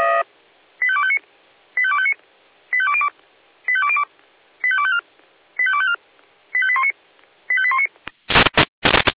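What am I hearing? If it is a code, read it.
Five-tone selective call (ZVEI), heard as quick runs of five tones, tones of about 70 ms. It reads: 85259 85259 85202 85202 85254 85254 85819 85819
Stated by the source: ZVEI five-tone selective call received by SDR.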